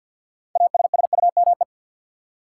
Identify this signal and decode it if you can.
Morse code, heard as a keyed tone, about 53 wpm, 700 Hz. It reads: ASSUME
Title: Sentences from Top 1500 Words - New Word Drill Alt - 50wpm